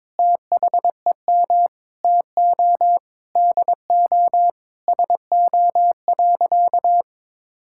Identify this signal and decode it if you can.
Morse code, heard as a keyed tone, about 22 wpm, 700 Hz. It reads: THEM TO DO SO.
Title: Sentences from Top 1000 Words - Review Part 5 - 22wpm